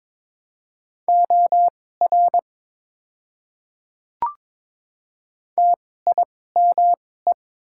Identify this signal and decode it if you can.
Morse code, heard as a keyed tone, about 21 wpm, spaced as for 16 wpm, 700 Hz. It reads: OR TIME
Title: 2x Character Spacing - Top 100 Words - 22wpm